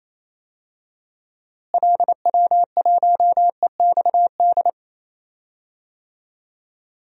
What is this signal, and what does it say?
Morse code, heard as a keyed tone, about 28 wpm, 700 Hz. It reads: LW1EXD
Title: DXCC-100 Call Signs - 6 Characters - Repeated 2x - 28wpm